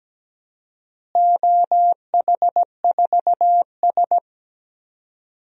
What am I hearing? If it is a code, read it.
Morse code, heard as a keyed tone, about 17 wpm, 700 Hz. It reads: OH4S
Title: DXCC-100 Call Signs - 4 Characters - Repeated 3x - 17wpm